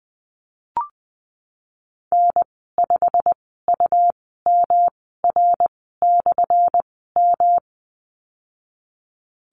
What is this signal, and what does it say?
Morse code, heard as a keyed tone, about 20 wpm, spaced as for 15 wpm, 700 Hz. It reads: N5UMR/M
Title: 179 Lesson 29g - Slash - Call Signs 20wpm